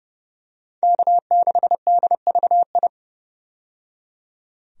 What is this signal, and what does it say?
Morse code, heard as a keyed tone, about 30 wpm, 700 Hz. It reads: K6DVI
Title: USA Call Signs - 5 Characters - 30wpm